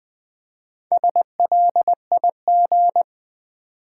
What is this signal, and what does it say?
Morse code, heard as a keyed tone, about 20 wpm, 700 Hz. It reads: SLIG
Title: Build A Word - 4 Letters At A Time - 8-Letter Words - 20wpm